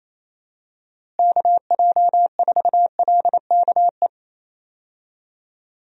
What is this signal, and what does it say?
Morse code, heard as a keyed tone, about 28 wpm, 700 Hz. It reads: KJ4LKE
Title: USA Call Signs - 6 Characters - Repeated 2x - 28wpm